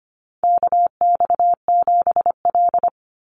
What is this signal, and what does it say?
Morse code, heard as a keyed tone, about 25 wpm, 700 Hz. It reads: KX7L
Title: NA SOTA Chaser Callsigns 25wpm